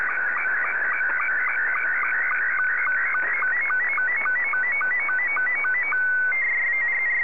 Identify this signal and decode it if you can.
SSTV (slow-scan television), Scottie 2.